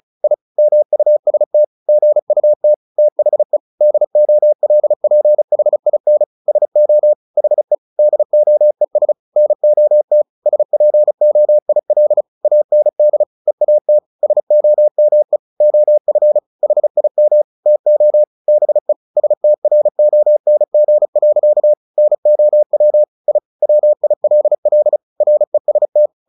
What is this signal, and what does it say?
Morse code, heard as a keyed tone, about 34 wpm, 600 Hz. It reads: I MUST GUT THE DOLPHIN SO HE DOES NOT SPOIL AND EAT SOME OF HIM TO BE STRONG. NOW I WILL REST
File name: the old man and the sea 2 35wpm